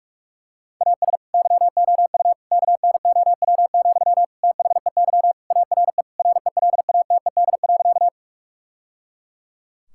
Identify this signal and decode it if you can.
Morse code, heard as a keyed tone, about 46 wpm, 700 Hz. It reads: AS YOU KNOW, THEY ARE RELATED.
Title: Sentences from Top 1000 Words - Review Part 5 - 45wpm